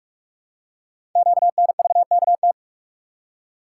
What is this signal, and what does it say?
Morse code, heard as a keyed tone, about 44 wpm, 700 Hz. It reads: QNVKT
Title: Code Groups - 5 Letter:Numbers - 45wpm